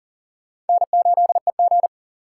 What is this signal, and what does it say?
Morse code, heard as a keyed tone, about 52 wpm, 700 Hz. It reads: N8EG